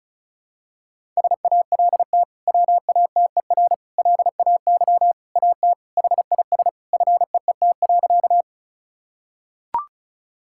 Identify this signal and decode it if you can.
Morse code, heard as a keyed tone, about 36 wpm, 700 Hz. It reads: SALT WATER LAY AT HIS FEET.
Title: Sentences from Top 1000 Words - Review Part 5 - 35wpm